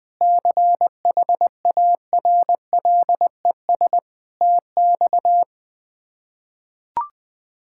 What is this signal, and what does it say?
Morse code, heard as a keyed tone, about 21 wpm, 700 Hz. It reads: CHARLES TX
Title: K1USN SST QSO Practice - 20wpm